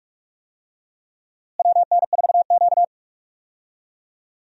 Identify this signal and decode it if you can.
Morse code, heard as a keyed tone, about 45 wpm, 700 Hz. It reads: WN4Q
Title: USA Call Signs - 4 Characters - Repeated 3x - 45wpm